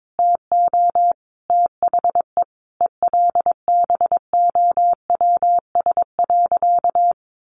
Morse code, heard as a keyed tone, about 22 wpm, 700 Hz. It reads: TO THE ELBOWS.
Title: A Sound Of Thunder 22wpm